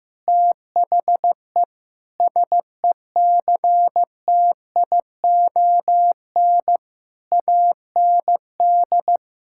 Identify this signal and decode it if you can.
Morse code, heard as a keyed tone, about 15 wpm, 700 Hz. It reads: THE SECTION AND
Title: Sentences from Top 1000 Words - Review Part 5 - 15wpm